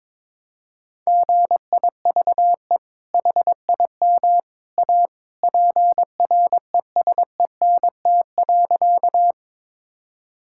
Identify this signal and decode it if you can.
Morse code, heard as a keyed tone, about 22 wpm, 700 Hz. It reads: GIVE HIM A PRESENT.
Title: Sentences from Top 1000 Words - Review Part 5 - 22wpm